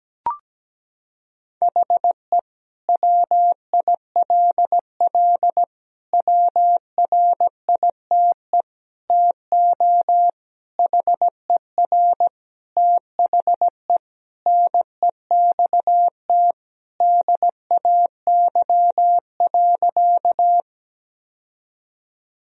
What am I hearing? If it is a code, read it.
Morse code, heard as a keyed tone, about 17 wpm, 700 Hz. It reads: HE WILL WRITE TO HER THE NEXT DAY.
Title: Sentences from Top 1000 Words - Review Part 1 - 17wpm